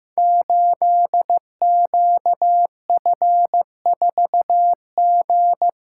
Morse code, heard as a keyed tone, about 15 wpm, 700 Hz. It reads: 8QF4G
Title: Code Groups - 5 Letter:Numbers - 15wpm